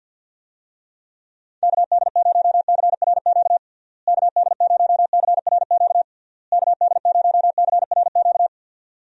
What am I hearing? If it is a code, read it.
Morse code, heard as a keyed tone, about 51 wpm, 700 Hz. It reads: KD0CRQ KD0CRQ KD0CRQ
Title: USA Call Signs - 6 Characters - Repeated 3x - 50wpm